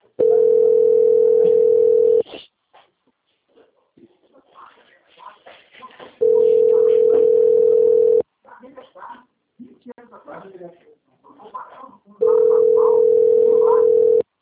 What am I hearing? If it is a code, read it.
Ringback tone (North America), 440 + 480 Hz, with speech.
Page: Americana - SP